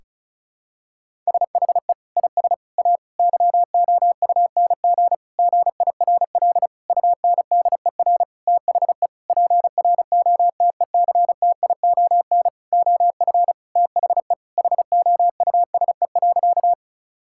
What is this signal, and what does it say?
Morse code, heard as a keyed tone, about 36 wpm, 700 Hz. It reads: SHE IS A YOUNG GIRL UNDER THE PROTECTION OF THE HOUSE.
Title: Sentences from Top 1600 Words 35wpm